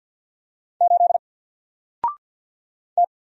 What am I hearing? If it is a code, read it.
Morse code, heard as a keyed tone, about 48 wpm, 700 Hz. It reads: 8 T
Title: Single Character (Letter:Number:Punctuation:Prosign) 50wpm